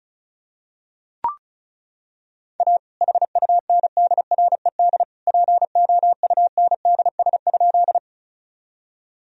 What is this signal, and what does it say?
Morse code, heard as a keyed tone, about 34 wpm, 700 Hz. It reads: A HUNDRED POUNDS?
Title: Sentences from Top 1000 Words - Review Part 2 - 35wpm